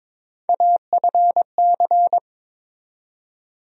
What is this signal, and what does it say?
Morse code, heard as a keyed tone, about 22 wpm, 700 Hz. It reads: AFC
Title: USA Call Sign Postfixes - 3 Letters 22wpm